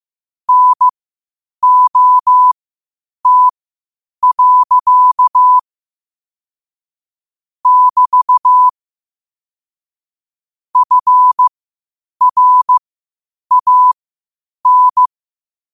Morse code, heard as a keyed tone, about 15 wpm, spaced as for 8 wpm, 1000 Hz. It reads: NOT. <BT> FRAN
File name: 8wpm